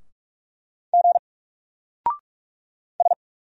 Morse code, heard as a keyed tone, about 46 wpm, 700 Hz. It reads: G S
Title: Single Character (Letter:Number:Punctuation:Prosign) 45wpm